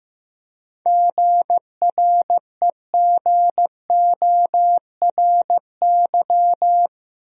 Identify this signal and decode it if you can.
Morse code, heard as a keyed tone, about 15 wpm, 700 Hz. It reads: GREGORY
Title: USA Common Names - 15wpm